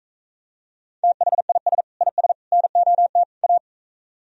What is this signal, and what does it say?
Morse code, heard as a keyed tone, about 45 wpm, 690 Hz. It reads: THIS IS NOT A